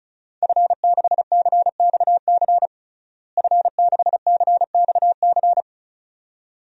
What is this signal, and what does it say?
Morse code, heard as a keyed tone, about 36 wpm, 700 Hz. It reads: F6CXC F6CXC
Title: DXCC-100 Call Signs - 5 Characters - Repeated 2x - 35wpm